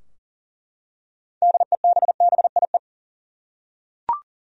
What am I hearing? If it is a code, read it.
Morse code, heard as a keyed tone, about 37 wpm, 700 Hz. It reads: DEBBIE
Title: USA Common Names - 40wpm